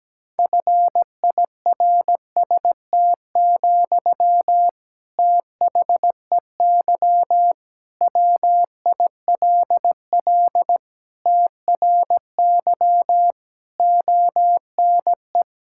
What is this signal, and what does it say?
Morse code, heard as a keyed tone, about 17 wpm, 700 Hz. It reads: FIRST, THEY WILL TRY ONE
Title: Sentences from Top 1000 Words - Review Part 6 - 17wpm